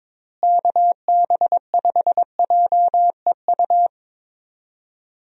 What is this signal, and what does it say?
Morse code, heard as a keyed tone, about 22 wpm, 700 Hz. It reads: KB5JEU
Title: USA Call Signs - 6 Characters - Repeated 2x - 22wpm